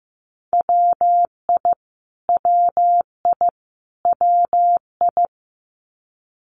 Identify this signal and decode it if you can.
Morse code, heard as a keyed tone, about 15 wpm, 700 Hz. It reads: WI WI WI
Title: US States Repeated 3x 15wpm